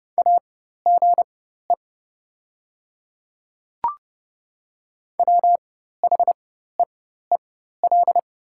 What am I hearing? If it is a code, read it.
Morse code, heard as a keyed tone, about 30 wpm, spaced as for 14 wpm, 700 Hz. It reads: AGE WHEEL
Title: 4x Character Spacing - Top 500 Words - 30wpm